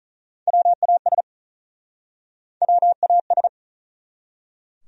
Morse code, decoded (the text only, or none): WAS WAS